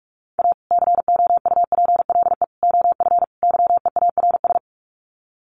Morse code, heard as a keyed tone, about 44 wpm, 700 Hz. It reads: A COUPLE OF YEARS
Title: Sets of 4 Words 45wpm